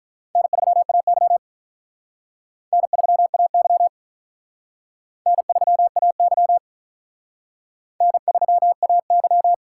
Morse code, heard as a keyed tone, about 42 wpm, 700 Hz. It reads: N3AY N3AY N3AY N3AY
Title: Speed Racing - USA 1x2 Call Signs 35wpm